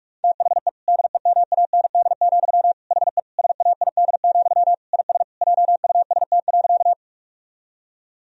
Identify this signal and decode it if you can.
Morse code, heard as a keyed tone, about 41 wpm, 700 Hz. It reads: THE DEMAND, HE SAID, IS JUST.
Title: Sentences from Top 1000 Words - Review Part 4 - 45wpm